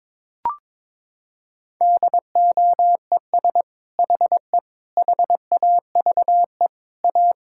Morse code, heard as a keyed tone, about 21 wpm, 700 Hz. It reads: DOES HE HAVE A
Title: Sentences from Top 1600 Words 22wpm